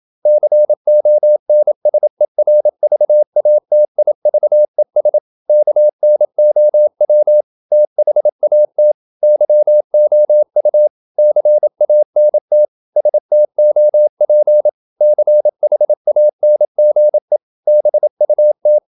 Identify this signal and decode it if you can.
Morse code, heard as a keyed tone, about 27 wpm, 600 Hz. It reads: CONSERVATIVES KNOW THAT YOU CANT STOP CHANGE BUT
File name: Never 0027wpm